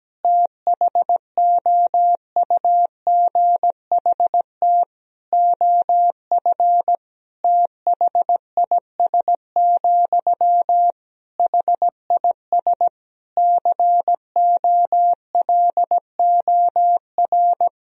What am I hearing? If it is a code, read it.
Morse code, heard as a keyed tone, about 17 wpm, 700 Hz. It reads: THOUGHT OF THIS, HIS COLOR
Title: Sentences from Top 1000 Words - Review Part 2 - 17wpm